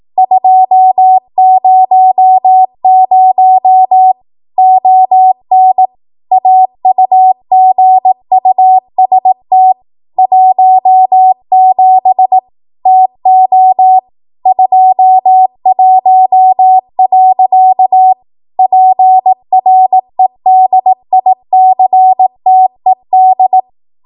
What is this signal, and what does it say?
Morse code, heard as a keyed tone, about 18 wpm, 750 Hz. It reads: 200 ON AUGUST 17 TO 21. PREDICTED
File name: Archive / 18 WPM / 18WPM